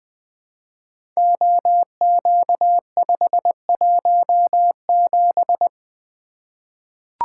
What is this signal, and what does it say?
Morse code, heard as a keyed tone, about 20 wpm, 700 Hz. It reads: OQ517
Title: Code Groups - 5 Letter:Numbers - 20wpm